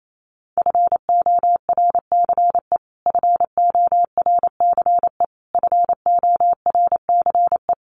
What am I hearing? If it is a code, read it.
Morse code, heard as a keyed tone, about 28 wpm, 700 Hz. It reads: FORCE FORCE FORCE
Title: Top 500 Words Repeated 28wpm